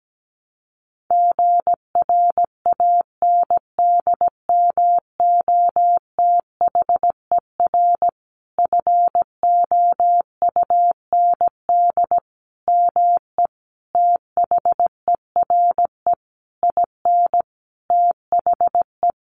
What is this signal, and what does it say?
Morse code, heard as a keyed tone, about 17 wpm, 700 Hz. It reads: GRANDMOTHER FOUND ME THERE IN THE
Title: Sentences from Top 1200 Words 17wpm